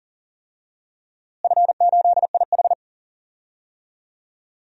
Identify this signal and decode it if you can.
Morse code, heard as a keyed tone, about 39 wpm, 700 Hz. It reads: F8IH
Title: DXCC-100 Call Signs - 4 Characters - Repeated 2x - 40wpm